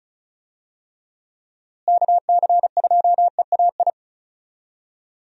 Morse code, heard as a keyed tone, about 36 wpm, 700 Hz. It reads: KC2EAI
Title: USA Call Signs - 6 Characters - Repeated 2x - 35wpm